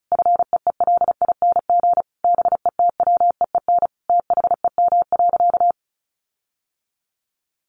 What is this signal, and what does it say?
Morse code, heard as a keyed tone, about 34 wpm, 700 Hz. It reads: FEELING BETWEEN THEM.
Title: Sentences from Top 1400 Words 35wpm